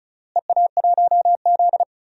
Morse code, heard as a keyed tone, about 34 wpm, 700 Hz. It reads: EA1Z
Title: DXCC-100 Call Signs - 4 Characters - Repeated 3x - 35wpm